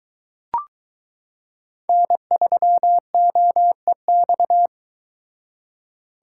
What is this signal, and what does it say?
Morse code, heard as a keyed tone, about 23 wpm, 700 Hz. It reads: N3OEX